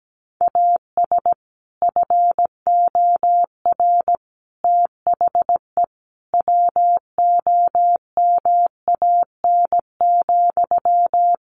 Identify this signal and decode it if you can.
Morse code, heard as a keyed tone, about 17 wpm, 700 Hz. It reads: AS FOR THE WOMAN,
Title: Sentences from Top 1100 Words 17wpm